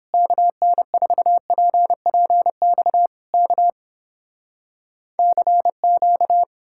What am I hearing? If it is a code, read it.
Morse code, heard as a keyed tone, about 31 wpm, 700 Hz. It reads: KN4PPX K CQ